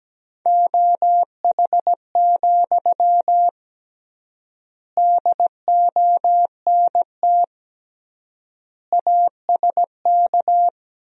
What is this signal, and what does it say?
Morse code, heard as a keyed tone, about 17 wpm, 700 Hz. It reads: OH, DONT ASK
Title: 3x Word-Spacing - Sentences from Top 500 Words - 17wpm